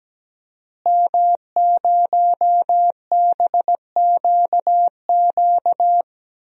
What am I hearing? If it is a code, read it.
Morse code, heard as a keyed tone, about 17 wpm, 700 Hz. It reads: M0BQQ
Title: DXCC-100 Call Signs - 5 Characters - Repeated 2x - 17wpm